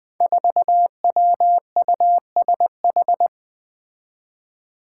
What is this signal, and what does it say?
Morse code, heard as a keyed tone, about 20 wpm, 700 Hz. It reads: 4WUSH
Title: Five Challenging Letter-Number Combinations 20wpm